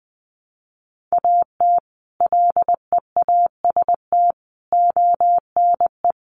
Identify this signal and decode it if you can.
Morse code, heard as a keyed tone, about 19 wpm, 700 Hz. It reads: AT LEAST ONE
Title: Sets of 5 Words Encore 20wpm